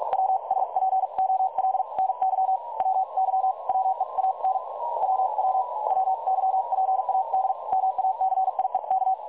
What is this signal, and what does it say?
Morse code, heard as a keyed tone, about 46 wpm, spaced as for 39 wpm, 760 Hz, with noise. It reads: WSPOLNY WY WE AN WW WYPAD MACIEK
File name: dlazaawansowanych / nagraniazpasma / websdr recording 3565.0kHz